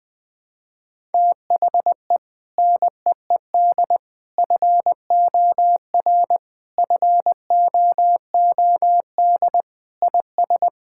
Morse code, heard as a keyed tone, about 21 wpm, 700 Hz. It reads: THE NEED FOR FOOD IS
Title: Sentences with 3-Word Phrases - Spoken Beginning - 20wpm